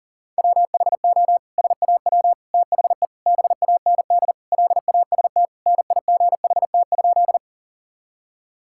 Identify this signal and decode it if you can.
Morse code, heard as a keyed tone, about 37 wpm, 700 Hz. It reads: WHO SAW THE BAND LAST NIGHT?
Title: Sentences from Top 1600 Words 40wpm